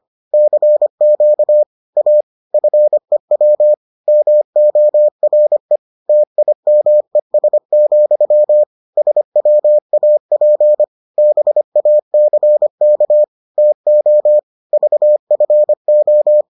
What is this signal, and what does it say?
Morse code, heard as a keyed tone, about 25 wpm, 600 Hz. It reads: CQ A FEW MORE TIMES, SWAP BACK TO VFO